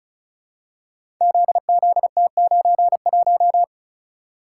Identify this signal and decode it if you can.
Morse code, heard as a keyed tone, about 36 wpm, 700 Hz. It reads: ZZT91
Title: Code Groups - 5 Letter:Numbers - 35wpm